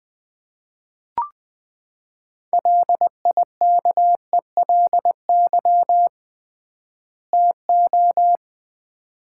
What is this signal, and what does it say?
Morse code, heard as a keyed tone, about 21 wpm, 700 Hz. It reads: LIKELY TO
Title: 3x Word-Spacing - Sets of 2 Words - 20wpm